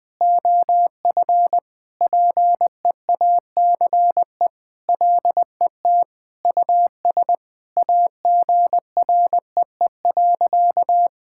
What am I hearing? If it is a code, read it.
Morse code, heard as a keyed tone, about 20 wpm, 700 Hz. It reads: OF PEACE LET US AGREE.
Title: Sentences with 4-Word Phrases - 20wpm